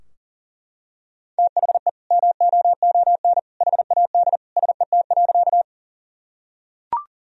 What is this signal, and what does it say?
Morse code, heard as a keyed tone, about 40 wpm, 700 Hz. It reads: THE MOON HAD SET.
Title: Sentences from Top 1000 Words - Review Part 3 - 40wpm